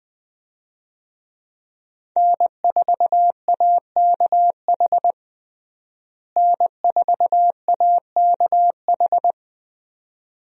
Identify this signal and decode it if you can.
Morse code, heard as a keyed tone, about 21 wpm, 700 Hz. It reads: N4AKH N4AKH
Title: USA Call Signs - 5 Characters - Repeated 3x - 20wpm